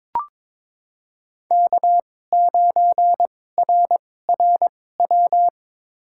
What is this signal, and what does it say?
Morse code, heard as a keyed tone, about 21 wpm, spaced as for 16 wpm, 700 Hz. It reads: K9RRW
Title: USA Call Signs - 5 Characters - 2x Character Spacing - 22wpm